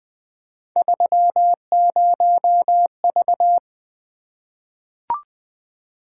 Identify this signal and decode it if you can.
Morse code, heard as a keyed tone, about 20 wpm, 700 Hz. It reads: 30V
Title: Three Letter-Number Combinations - 20wpm